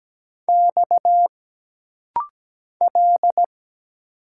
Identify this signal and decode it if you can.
Morse code, heard as a keyed tone, about 17 wpm, 700 Hz. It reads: X L
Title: Single Letters - Rapid-Fire 17wpm